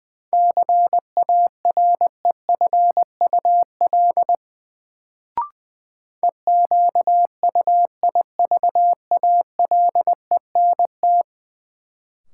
Morse code, heard as a keyed tone, about 20 wpm, 700 Hz. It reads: CAREFUL EQUIVALENT
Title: Sentences from Top 1600 Words - New Word Drill Alt - 20wpm